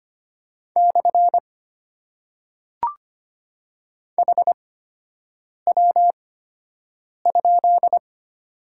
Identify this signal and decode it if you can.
Morse code, heard as a keyed tone, about 25 wpm, spaced as for 7 wpm, 700 Hz. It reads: / HW?
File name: Top 90 QSO Elements - 8x Character Spacing - 25wpm